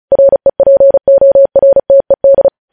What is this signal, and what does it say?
Morse code, heard as a keyed tone, about 36 wpm, 550 Hz. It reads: REPORTED
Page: News Headlines in Morse Code at 35 WPM